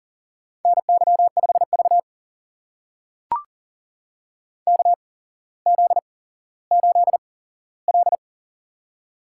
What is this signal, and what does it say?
Morse code, heard as a keyed tone, about 41 wpm, spaced as for 17 wpm, 690 Hz. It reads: NY5V KZ8L
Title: USA Call Signs - 4 Characters - 8x Character Spacing - 40wpm